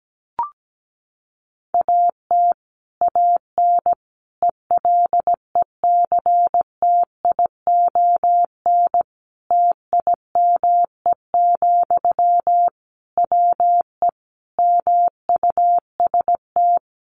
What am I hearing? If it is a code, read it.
Morse code, heard as a keyed tone, about 17 wpm, 700 Hz. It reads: AT AN ELECTION TIME, WE MUST
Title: Sentences from Top 1500 Words 17wpm